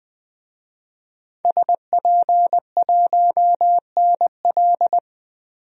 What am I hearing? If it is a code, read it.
Morse code, heard as a keyed tone, about 20 wpm, 700 Hz. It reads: SP1NL
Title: DXCC-100 Call Signs - 5 Characters - 20wpm